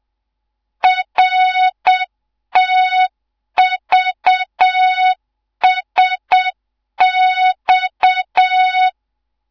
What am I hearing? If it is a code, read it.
Morse code, heard as a keyed tone, about 7 wpm, 3700 Hz, with noise. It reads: RTVSX